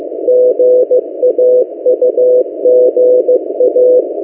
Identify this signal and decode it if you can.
Morse code, heard as a keyed tone, about 15 wpm, 500 Hz. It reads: GAUGA
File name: M08a- -1355utc-8096khz